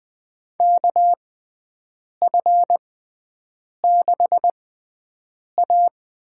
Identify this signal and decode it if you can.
Morse code, heard as a keyed tone, about 20 wpm, spaced as for 7 wpm, 700 Hz. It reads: KF6A
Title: USA Call Signs - 6 Characters - 6x Character Spacing - 20wpm